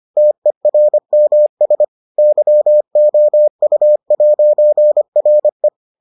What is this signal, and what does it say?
Morse code, heard as a keyed tone, about 25 wpm, 600 Hz. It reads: TERMS YOU'RE